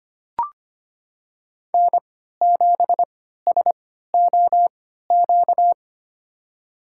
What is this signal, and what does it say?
Morse code, heard as a keyed tone, about 25 wpm, spaced as for 14 wpm, 700 Hz. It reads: N7SOQ
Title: USA Call Signs - 5 Characters - 3x Character Spacing - 25wpm